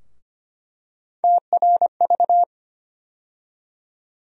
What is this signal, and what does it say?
Morse code, heard as a keyed tone, about 25 wpm, 700 Hz. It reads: TRV
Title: USA Call Sign Postfixes - 3 Letters 25wpm